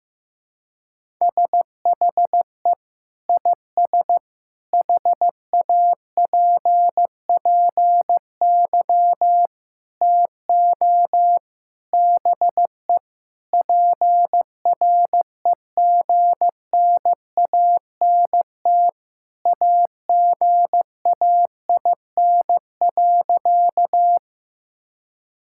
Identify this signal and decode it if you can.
Morse code, heard as a keyed tone, about 15 wpm, 700 Hz. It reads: SHE IS HAPPY TO BE PREGNANT AGAIN.
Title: Sentences from Top 1000 Words - Review Part 6 - 15wpm